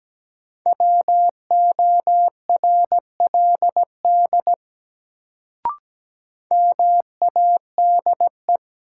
Morse code, heard as a keyed tone, about 17 wpm, 700 Hz. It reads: WORLD MADE
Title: Sentences from Top 200 Words - New Word Drill - 17wpm